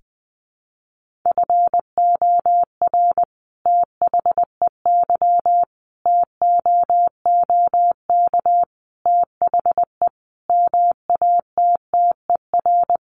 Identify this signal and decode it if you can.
Morse code, heard as a keyed tone, about 20 wpm, 700 Hz. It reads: FOR THEY TOOK THE MATTER
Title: Sentences from Top 1400 Words 20wpm